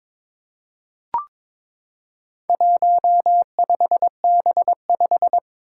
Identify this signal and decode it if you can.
Morse code, heard as a keyed tone, about 22 wpm, 700 Hz. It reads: 15B5
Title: Four Challenging Letter-Number Combinations 22wpm